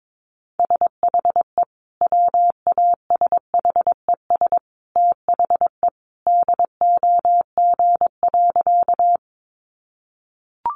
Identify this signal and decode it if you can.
Morse code, heard as a keyed tone, about 22 wpm, 700 Hz. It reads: SHE WASHES THE DOG.
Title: Easy Phrases 22wpm